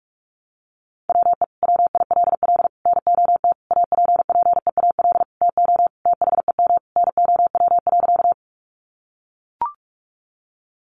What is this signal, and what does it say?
Morse code, heard as a keyed tone, about 44 wpm, 700 Hz. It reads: WE WILL NOT APPEAL TO THEM NOW.
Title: Sentences from Top 1300 Words 45wpm